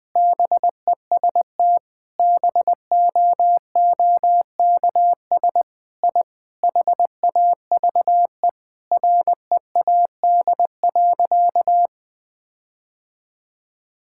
Morse code, heard as a keyed tone, about 21 wpm, 700 Hz. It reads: BEST BOOKS I HAVE READ.
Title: Sentences with 5-Word Phrases - 20wpm